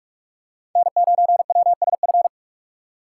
Morse code, heard as a keyed tone, about 44 wpm, 700 Hz. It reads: N9WSF
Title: USA Call Signs - 5 Characters - 4x Character Spacing - 45wpm